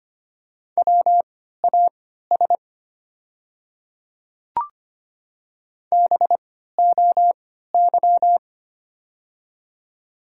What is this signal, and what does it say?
Morse code, heard as a keyed tone, about 25 wpm, spaced as for 14 wpm, 700 Hz. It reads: WAS BOY
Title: 3x Character Spacing - Top 300 Words - 25wpm